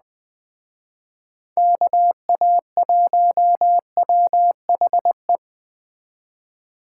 Morse code, heard as a keyed tone, about 20 wpm, 700 Hz. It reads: KA1WHE
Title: Speed Racing - USA 2x3 Call Signs 17wpm